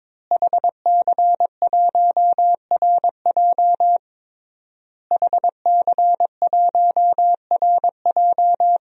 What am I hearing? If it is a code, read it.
Morse code, heard as a keyed tone, about 22 wpm, 700 Hz. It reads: HC1RJ HC1RJ
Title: DXCC-100 Call Signs - 5 Characters - Repeated 3x - 22wpm